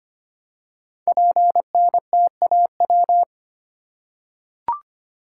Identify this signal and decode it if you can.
Morse code, heard as a keyed tone, about 25 wpm, 700 Hz. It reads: PNTAW
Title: Code Groups - 5 Letters - 25wpm